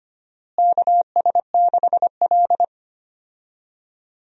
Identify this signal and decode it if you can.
Morse code, heard as a keyed tone, about 25 wpm, 700 Hz. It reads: KS6L
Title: USA Call Signs - 4 Characters - Repeated 2x - 25wpm